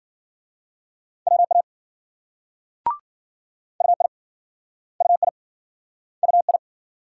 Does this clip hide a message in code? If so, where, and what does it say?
Yes: Morse code from 1.3 to 6.6 s, heard as a keyed tone, about 54 wpm, 700 Hz. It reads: WA UI UI UI